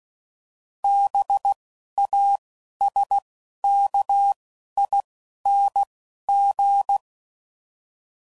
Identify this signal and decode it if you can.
Morse code, heard as a keyed tone, about 16 wpm, 800 Hz. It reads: B A S K I N G
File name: Suffixes / Special INGs / Single INGs